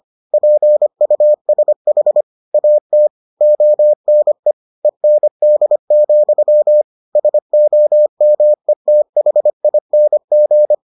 Morse code, heard as a keyed tone, about 25 wpm, 600 Hz. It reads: PUSH AT ONE END, SOMETHING